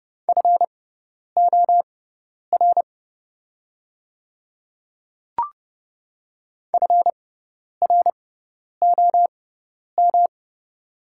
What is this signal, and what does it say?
Morse code, heard as a keyed tone, about 30 wpm, spaced as for 10 wpm, 700 Hz. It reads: FOR FROM
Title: 6x Character Spacing - Top 100 Words - 30wpm